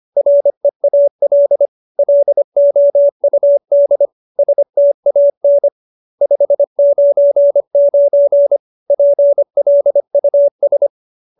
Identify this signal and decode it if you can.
Morse code, heard as a keyed tone, about 25 wpm, 570 Hz. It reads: REAL LOUD STAN 599 PLUS